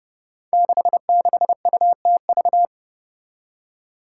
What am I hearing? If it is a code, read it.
Morse code, heard as a keyed tone, about 30 wpm, 700 Hz. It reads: 66UTV
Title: Five Challenging Letter-Number Combinations 30wpm